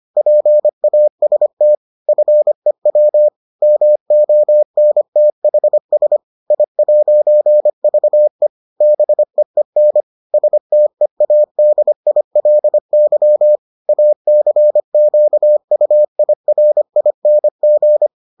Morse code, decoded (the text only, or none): PAST FEW MONTHS I'VE BEEN STEADILY ACQUIRING